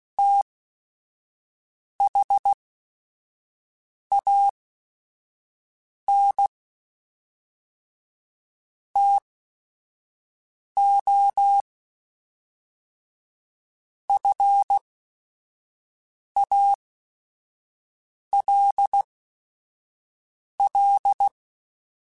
Morse code, heard as a keyed tone, about 16 wpm, spaced as for 5 wpm, 800 Hz. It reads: THAN TO FALL
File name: Quotes / Quotes-mp3s / StandFor @5